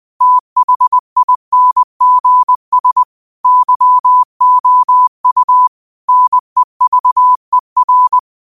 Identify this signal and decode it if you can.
Morse code, heard as a keyed tone, about 20 wpm, 1000 Hz. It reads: THINGS YOU NEVER